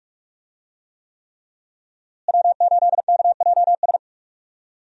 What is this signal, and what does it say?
Morse code, heard as a keyed tone, about 41 wpm, 700 Hz. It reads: W8KJS
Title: USA Call Signs - 5 Characters - Repeated 3x - 45wpm